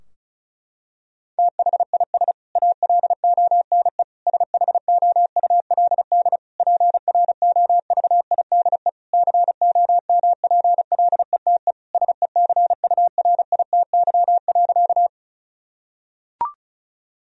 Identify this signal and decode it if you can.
Morse code, heard as a keyed tone, about 36 wpm, 700 Hz. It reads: THIS ALONE SHOULD PROVIDE COMPLETE SECURITY.
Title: Sentences from Top 1000 Words - Review Part 4 - 35wpm